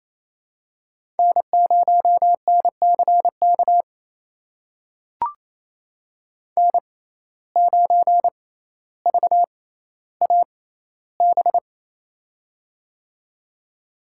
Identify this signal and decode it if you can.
Morse code, heard as a keyed tone, about 28 wpm, spaced as for 14 wpm, 700 Hz. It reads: N0NCK N9VAB
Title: USA Call Signs - 5 Characters - 6x Character Spacing - 28wpm